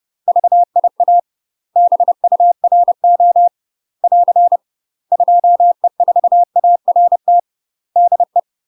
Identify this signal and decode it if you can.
Morse code, heard as a keyed tone, about 30 wpm, 700 Hz. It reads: VIA BURO <AR> 2E4ART DE
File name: UK MASTER QSOs TRAINING / 30WPM / UK MASTER QSOs 0001